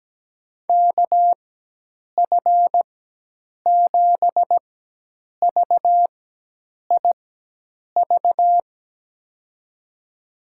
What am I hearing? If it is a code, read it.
Morse code, heard as a keyed tone, about 17 wpm, spaced as for 8 wpm, 700 Hz. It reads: KF7VIV
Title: USA Call Signs - 6 Characters - 4x Character Spacing - 17wpm